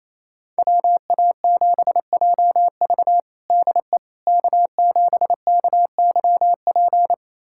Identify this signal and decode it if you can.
Morse code, heard as a keyed tone, about 28 wpm, 700 Hz. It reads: WA7JV DE K7KYP
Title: USA Callsign DE Callsign - 5 Characters - 28wpm